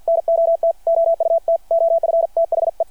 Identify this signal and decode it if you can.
Morse code, heard as a keyed tone, about 52 wpm, 660 Hz. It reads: NOT OUT OF THE
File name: archived / CRSnet